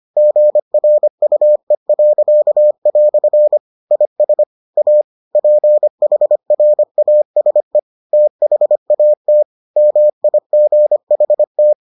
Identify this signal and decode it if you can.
Morse code, heard as a keyed tone, about 25 wpm, 600 Hz. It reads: GRUE." IS A PHRASE THAT MIGHT